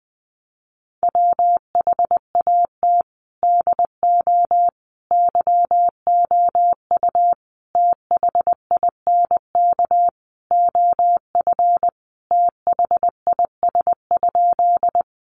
Morse code, heard as a keyed tone, about 19 wpm, 700 Hz. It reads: WHAT DO YOU THINK OF THIS?
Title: Sentences from Top 200 Words 20wpm